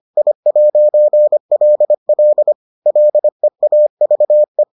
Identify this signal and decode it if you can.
Morse code, heard as a keyed tone, about 25 wpm, 600 Hz. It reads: I'LL LEAVE